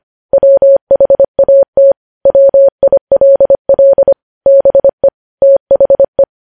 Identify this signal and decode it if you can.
Morse code, heard as a keyed tone, about 25 wpm, 550 Hz. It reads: WHAT WILL BE THE
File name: podcast / News headlines-1x-25x25